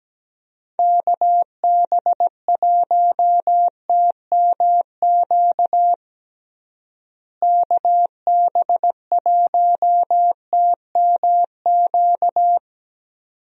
Morse code, heard as a keyed tone, about 17 wpm, 700 Hz. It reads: KB1TMQ KB1TMQ
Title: USA Call Signs - 6 Characters - Repeated 2x - 17wpm